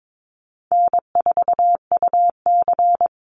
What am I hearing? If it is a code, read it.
Morse code, heard as a keyed tone, about 22 wpm, 700 Hz. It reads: N4UC
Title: Call Signs - 22wpm